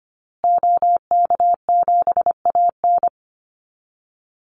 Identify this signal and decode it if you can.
Morse code, heard as a keyed tone, about 25 wpm, 700 Hz. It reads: OK7AN
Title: Call Signs Repeated 2x - 25wpm